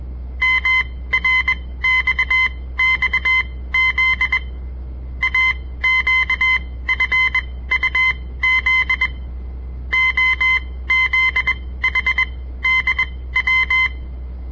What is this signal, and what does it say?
Morse code, heard as a keyed tone, about 21 wpm, spaced as for 15 wpm, 2050 Hz. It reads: MRXXZ AQFUZ OZHDW
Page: On the air Morse code practice from the Southern Montana Amatuer Radio Association; all code, no read back